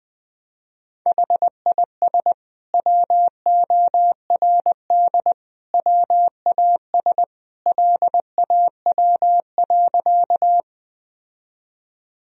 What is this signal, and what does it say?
Morse code, heard as a keyed tone, about 20 wpm, 700 Hz. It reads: HIS WORD WAS LAW.
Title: Sentences from Top 1000 Words - Review Part 3 - 20wpm